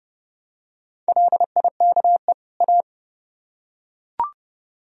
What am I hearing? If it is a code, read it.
Morse code, heard as a keyed tone, about 29 wpm, 700 Hz. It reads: LIKE A